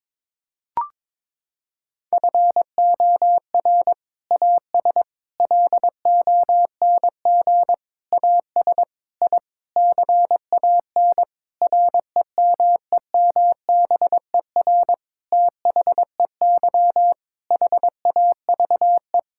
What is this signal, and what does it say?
Morse code, heard as a keyed tone, about 22 wpm, 700 Hz. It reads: FOR AS LONG AS I CAN REMEMBER THEY HAVE
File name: Sentences with 7-Word Phrases - 22wpm